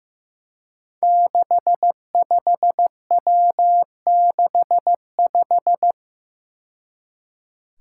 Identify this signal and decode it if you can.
Morse code, heard as a keyed tone, about 15 wpm, 700 Hz. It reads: 65W65
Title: Five Challenging Letter-Number Combinations 15wpm